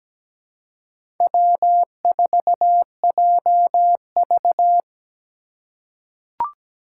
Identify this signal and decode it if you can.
Morse code, heard as a keyed tone, about 17 wpm, 700 Hz. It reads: W4JV
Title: USA Call Signs - 4 Characters - 3x Character Spacing - 17wpm